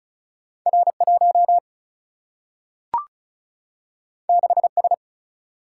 Morse code, heard as a keyed tone, about 36 wpm, 700 Hz. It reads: R1 6S
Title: Single-Letter Single-Number 35wpm